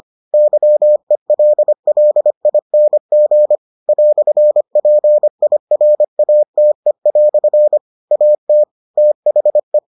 Morse code, heard as a keyed tone, about 25 wpm, 600 Hz. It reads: YELLING "PIRATE" AT THE